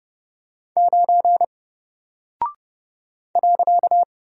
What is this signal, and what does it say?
Morse code, heard as a keyed tone, about 29 wpm, 700 Hz. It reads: 9 .
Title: Single Character (Letter:Number:Punctuation:Prosign) 30wpm